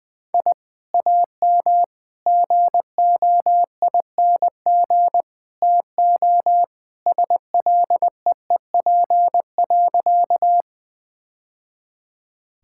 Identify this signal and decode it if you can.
Morse code, heard as a keyed tone, about 20 wpm, 700 Hz. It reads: I AM GOING TO SLEEP.
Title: Sentences with 2-Word Phrases - 20wpm